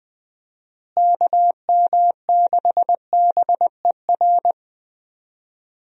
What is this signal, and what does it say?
Morse code, heard as a keyed tone, about 20 wpm, 700 Hz. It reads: KM6BER
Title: USA Call Signs - 6 Characters - Repeated 3x - 20wpm